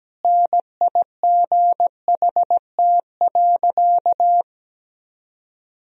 Morse code, heard as a keyed tone, about 17 wpm, 700 Hz. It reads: NIGHT.
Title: Sentences from Top 500 Words and Abbreviations 17wpm